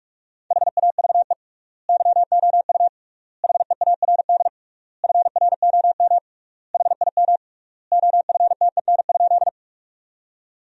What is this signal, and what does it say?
Morse code, heard as a keyed tone, about 46 wpm, 700 Hz. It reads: HAVE YOU HEARD FROM HIM OFTEN?
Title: 3x Word-Spacing - Sentences from Top 400 Words - 45wpm